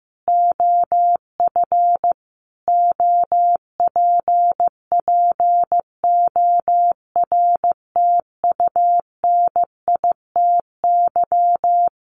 Morse code, heard as a keyed tone, about 15 wpm, 700 Hz. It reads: OF OPPORTUNITY
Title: Sentences from Top 1300 Words 15wpm